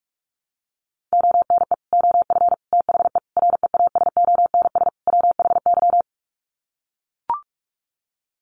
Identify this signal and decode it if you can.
Morse code, heard as a keyed tone, about 44 wpm, 700 Hz. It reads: ONE OF THE REASONS WHY
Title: Sets of 5 Words 45wpm